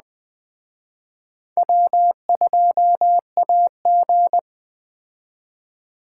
Morse code, heard as a keyed tone, about 20 wpm, 700 Hz. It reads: W2AG
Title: Speed Racing - USA 1x2 Call Signs 17wpm